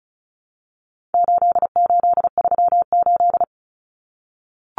Morse code, heard as a keyed tone, about 34 wpm, 700 Hz. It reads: 8838
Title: Numbers - 4-Digit 35wpm